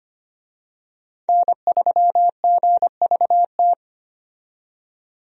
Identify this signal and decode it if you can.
Morse code, heard as a keyed tone, about 25 wpm, 700 Hz. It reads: N3GVT
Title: USA Call Signs - 5 Characters - 25wpm